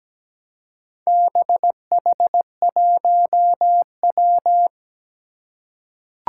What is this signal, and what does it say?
Morse code, heard as a keyed tone, about 17 wpm, 700 Hz. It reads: BH1W
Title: Four Challenging Letter-Number Combinations 17wpm